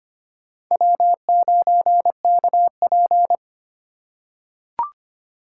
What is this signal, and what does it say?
Morse code, heard as a keyed tone, about 25 wpm, 700 Hz. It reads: W9KP
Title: USA Call Signs - 2x Character Spacing - 25wpm